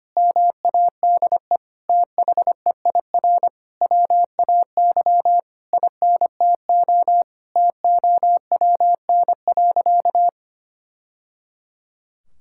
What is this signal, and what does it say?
Morse code, heard as a keyed tone, about 25 wpm, 700 Hz. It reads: MADE THEIR WAY INTO TOWN.
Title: Sentences from Top 1000 Words - Review Part 6 - 25wpm